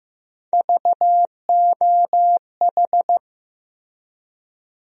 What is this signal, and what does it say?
Morse code, heard as a keyed tone, about 15 wpm, 700 Hz. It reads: VOH
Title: USA Call Sign Postfixes - 3 Letters 15wpm